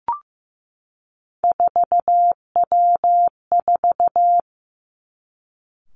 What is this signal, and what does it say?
Morse code, heard as a keyed tone, about 15 wpm, 700 Hz. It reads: 4W4
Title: Three Challenging Letter-Number Combinations 15wpm